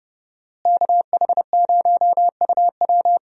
Morse code, heard as a keyed tone, about 31 wpm, 700 Hz. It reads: KH0UW
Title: USA Call Signs - 5 Characters - Repeated 3x - 30wpm